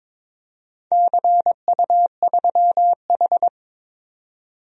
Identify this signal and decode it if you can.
Morse code, heard as a keyed tone, about 22 wpm, 700 Hz. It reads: CU3H
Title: DXCC-100 Call Signs - 4 Characters - 22wpm